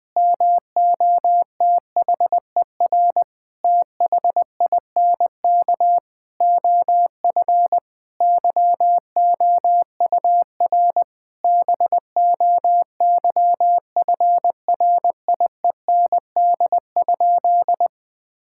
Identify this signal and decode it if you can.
Morse code, heard as a keyed tone, about 19 wpm, 700 Hz. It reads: MOTHER THINK OF YOUR BOYFRIEND?
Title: Sentences from Top 1000 Words - Review Part 5 - 20wpm